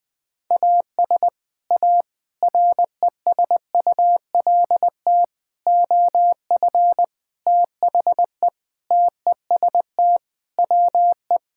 Morse code, heard as a keyed tone, about 21 wpm, 700 Hz. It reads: AS A RESULT OF THE TEST WE
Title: Sentences with 4-Word Phrases - 20wpm